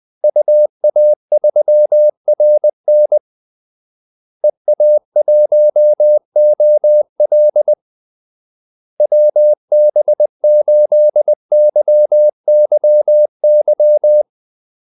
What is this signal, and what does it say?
Morse code, heard as a keyed tone, about 20 wpm, 600 Hz. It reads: UA3RN EA1OL WB8YYY
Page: Kaldesignaler 20 wpm